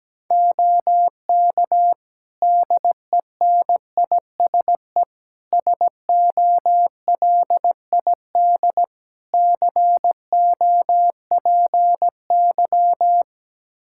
Morse code, heard as a keyed tone, about 17 wpm, 700 Hz. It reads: OK DENISE SOLID COPY
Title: K1USN SST QSO Practice 17wpm